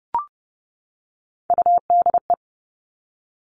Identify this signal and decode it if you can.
Morse code, heard as a keyed tone, about 30 wpm, 700 Hz. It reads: UDE
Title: All Cut Numbers - 3-Digit 30wpm